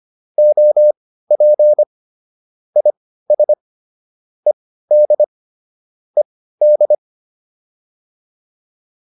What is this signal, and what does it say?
Morse code, heard as a keyed tone, about 25 wpm, spaced as for 15 wpm, 600 Hz. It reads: OP IS ED ED
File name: QSO 302 15